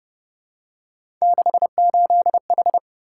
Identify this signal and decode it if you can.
Morse code, heard as a keyed tone, about 31 wpm, 700 Hz. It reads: 68H